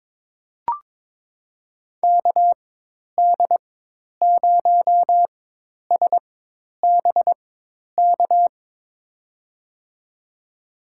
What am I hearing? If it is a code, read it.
Morse code, heard as a keyed tone, about 22 wpm, spaced as for 10 wpm, 700 Hz. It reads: KD0SBK
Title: USA Call Signs - 6 Characters - 4x Character Spacing - 22wpm